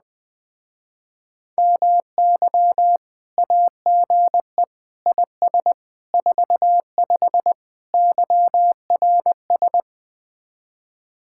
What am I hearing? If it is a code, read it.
Morse code, heard as a keyed tone, about 20 wpm, 700 Hz. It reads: MY AGE IS 45 YRS